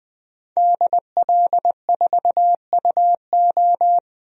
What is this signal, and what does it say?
Morse code, heard as a keyed tone, about 20 wpm, 700 Hz. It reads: DL4UO
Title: DXCC-100 Call Signs - 5 Characters - 20wpm